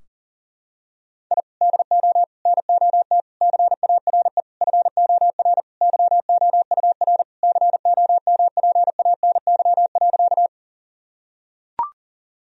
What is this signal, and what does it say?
Morse code, heard as a keyed tone, about 40 wpm, 700 Hz. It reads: I DO NOT CARE FOR YOUR COMPANY.
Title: Sentences from Top 1000 Words - Review Part 3 - 40wpm